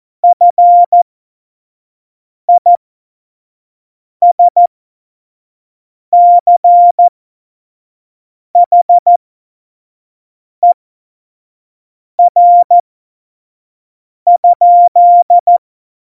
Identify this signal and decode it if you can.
Morse code, heard as a keyed tone, about 14 wpm, spaced as for 5 wpm, 700 Hz. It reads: FISCHER?
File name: Morsekurs / Witze / Witz 29. September 2020, 25 F70